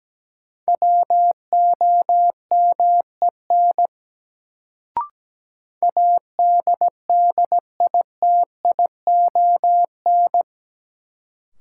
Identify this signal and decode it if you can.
Morse code, heard as a keyed tone, about 17 wpm, 700 Hz. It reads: WOMEN ADDITION
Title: Sets of 3 Words - New Word Drill - 17wpm